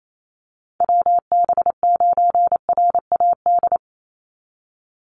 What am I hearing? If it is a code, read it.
Morse code, heard as a keyed tone, about 28 wpm, 700 Hz. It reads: WB9RAD
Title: USA Call Signs 28wpm